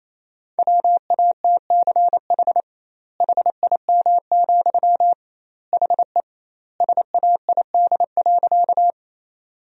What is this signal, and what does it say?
Morse code, heard as a keyed tone, about 28 wpm, 700 Hz. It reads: WATCH HIM, HE SAID.
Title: 2x Word-Spacing - Sentences from Top 300 Words - 28wpm